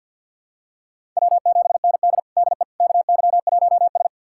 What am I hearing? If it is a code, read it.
Morse code, heard as a keyed tone, about 51 wpm, 700 Hz. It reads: W7ND DE KY1S
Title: USA Callsign DE Callsign - 4 Characters - 50wpm